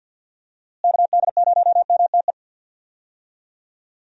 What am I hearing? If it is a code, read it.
Morse code, heard as a keyed tone, about 45 wpm, 700 Hz. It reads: KD0MTE
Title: USA Call Signs - 6 Characters - Repeated 2x - 50wpm